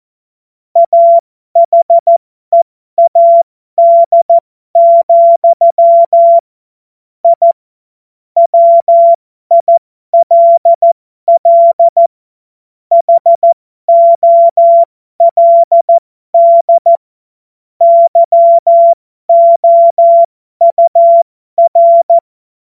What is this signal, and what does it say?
Morse code, heard as a keyed tone, about 14 wpm, spaced as for 12 wpm, 680 Hz. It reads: AHEAD, I WILL HOLD YOUR